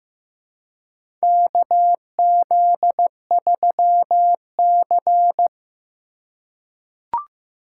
Morse code, heard as a keyed tone, about 15 wpm, 700 Hz. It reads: KZ3C
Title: USA Call Signs - 2x Character Spacing - 15wpm